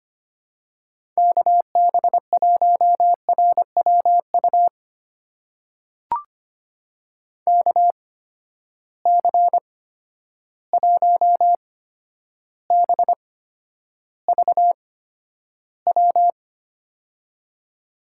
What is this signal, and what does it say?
Morse code, heard as a keyed tone, about 25 wpm, spaced as for 11 wpm, 700 Hz. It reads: KB1RWU KC1BVW
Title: USA Call Signs - 6 Characters - 8x Character Spacing - 25wpm